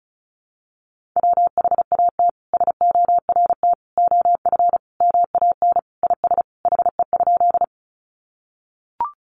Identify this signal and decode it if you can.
Morse code, heard as a keyed tone, about 34 wpm, 700 Hz. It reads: WHAT SORT OF MAN IS HE?
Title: Sentences from Top 1300 Words 35wpm